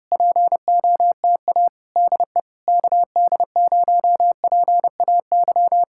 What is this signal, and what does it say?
Morse code, heard as a keyed tone, about 26 wpm, 700 Hz. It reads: POTA DE KD0PAY
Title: Speed-Racing Common QSO Phrases - 25wpm